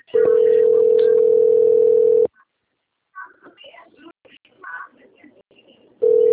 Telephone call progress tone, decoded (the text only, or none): ringback tone